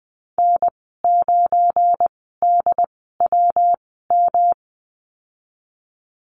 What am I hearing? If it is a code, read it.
Morse code, heard as a keyed tone, about 20 wpm, spaced as for 15 wpm, 700 Hz. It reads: N9DWM
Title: 173 Lesson 28g - Number 9 - Call Signs 20wpm